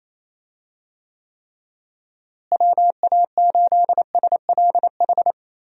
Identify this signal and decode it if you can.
Morse code, heard as a keyed tone, about 28 wpm, 700 Hz. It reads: WA8SLH
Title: USA Call Signs - 6 Characters - Repeated 3x - 28wpm